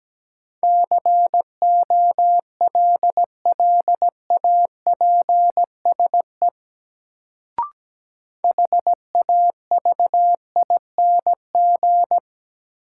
Morse code, heard as a keyed tone, about 17 wpm, 700 Hz. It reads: COLLAPSE HAVING
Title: Sets of 5 Words Encore - New Word Drill - 17wpm